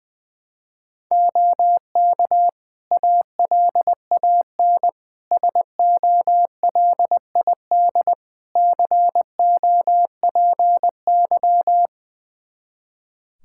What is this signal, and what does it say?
Morse code, heard as a keyed tone, about 21 wpm, 700 Hz. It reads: OK ALAN SOLID COPY
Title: Practice QSO - Exchange 3 - More Info - 20wpm